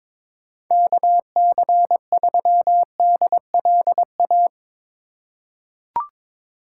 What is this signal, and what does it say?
Morse code, heard as a keyed tone, about 22 wpm, 700 Hz. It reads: KC3DLA
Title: USA Call Signs - 6 Characters - 2x Character Spacing - 22wpm